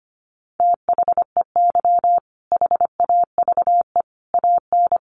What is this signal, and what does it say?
Morse code, heard as a keyed tone, about 25 wpm, 700 Hz. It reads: THEY HAVE AN
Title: Sentences from Top 1000 Words 25wpm